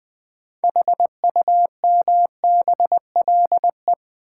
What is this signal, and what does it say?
Morse code, heard as a keyed tone, about 20 wpm, 700 Hz. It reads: HUMBLE
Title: Build A Word - 2 Letters At A Time - 6-Letter Words - 20wpm